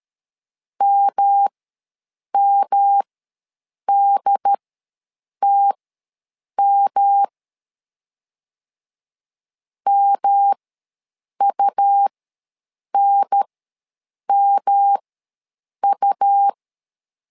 Morse code, heard as a keyed tone, about 13 wpm, spaced as for 7 wpm, 800 Hz. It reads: MMDTM MUNMU
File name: LEZIONI / 3 MWRO / 3 M ws 12wpm 1745